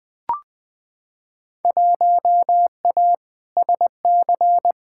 Morse code, heard as a keyed tone, about 20 wpm, 700 Hz. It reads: 1A SC
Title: Field Day Exchanges - 20wpm